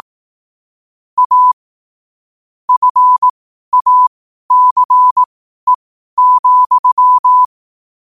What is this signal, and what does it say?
Morse code, heard as a keyed tone, about 18 wpm, spaced as for 12 wpm, 1000 Hz. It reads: A FACE,
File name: qotdfiles / 12wpm